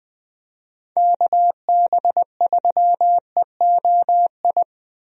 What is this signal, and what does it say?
Morse code, heard as a keyed tone, about 20 wpm, 700 Hz. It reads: KB3EOI
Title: USA Call Signs - 6 Characters - 20wpm